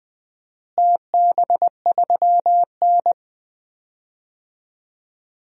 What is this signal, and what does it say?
Morse code, heard as a keyed tone, about 20 wpm, 700 Hz. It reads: TB3N
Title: DXCC-100 Call Signs - 4 Characters - Repeated 2x - 20wpm